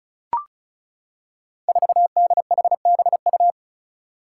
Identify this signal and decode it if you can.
Morse code, heard as a keyed tone, about 36 wpm, 700 Hz. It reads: 4DHBU
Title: Five Challenging Letter-Number Combinations 35wpm